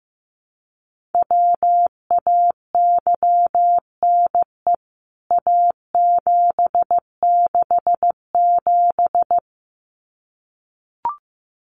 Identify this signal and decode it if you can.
Morse code, heard as a keyed tone, about 15 wpm, 700 Hz. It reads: WAYNE A767
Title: CWT Exchange 15wpm